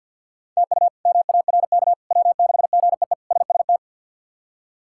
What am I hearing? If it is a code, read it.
Morse code, heard as a keyed tone, about 50 wpm, 700 Hz. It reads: TU MARK W6GEE SST
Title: K1USN SST QSO Practice - 50wpm